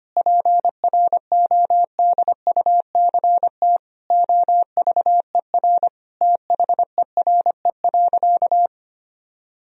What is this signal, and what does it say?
Morse code, heard as a keyed tone, about 25 wpm, 700 Hz. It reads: PRODUCT OVER THERE.
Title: Sentences from Top 1000 Words - Review Part 2 - 25wpm